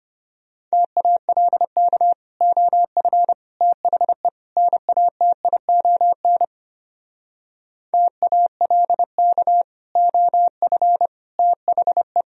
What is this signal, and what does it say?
Morse code, heard as a keyed tone, about 31 wpm, 700 Hz. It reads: TALK OF THE NATION TALK OF THE
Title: Speed-Racing - Sets of 4 Words 25wpm